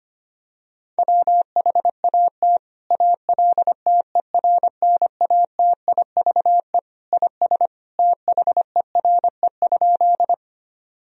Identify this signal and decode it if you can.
Morse code, heard as a keyed tone, about 25 wpm, 700 Hz. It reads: WHAT ALTERNATIVE IS THERE?
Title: Sentences from Top 1000 Words - Review Part 4 - 25wpm